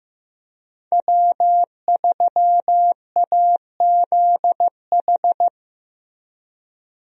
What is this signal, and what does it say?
Morse code, heard as a keyed tone, about 15 wpm, 700 Hz. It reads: W3AZH
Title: USA Call Signs - 5 Characters - 15wpm